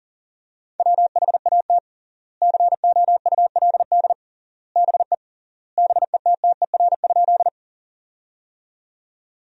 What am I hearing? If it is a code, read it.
Morse code, heard as a keyed tone, about 40 wpm, 700 Hz. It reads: WHAT COULD BE BETTER?
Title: 3x Word-Spacing - Sentences from Top 400 Words - 40wpm